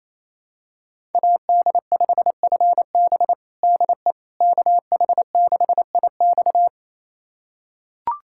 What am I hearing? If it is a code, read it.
Morse code, heard as a keyed tone, about 27 wpm, 700 Hz. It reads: AD5FB DE KH6IX
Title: USA Callsign DE Callsign - 5 Characters - 28wpm